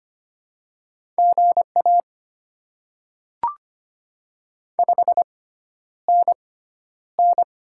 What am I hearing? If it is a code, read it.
Morse code, heard as a keyed tone, about 25 wpm, spaced as for 11 wpm, 700 Hz. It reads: GA 5NN
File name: Top 20 QSO Elements - 6x Character Spacing - 25wpm